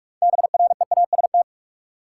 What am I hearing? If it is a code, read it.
Morse code, heard as a keyed tone, about 47 wpm, 700 Hz. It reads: BREAST
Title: Six Letter Words - 45wpm